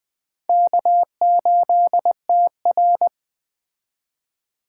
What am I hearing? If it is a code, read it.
Morse code, heard as a keyed tone, about 20 wpm, 700 Hz. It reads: K8TR
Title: Speed Racing - USA 1x2 Call Signs 17wpm